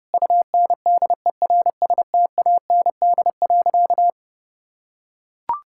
Morse code, heard as a keyed tone, about 29 wpm, 700 Hz. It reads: UNDERSTAND.
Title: Sentences from Top 1000 Words - Review Part 6 - 30wpm